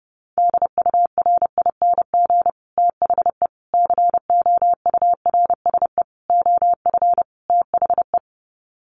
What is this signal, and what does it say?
Morse code, heard as a keyed tone, about 30 wpm, 700 Hz. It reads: DURING THE COURSE OF THE
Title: Sets of 5 Words Encore 30wpm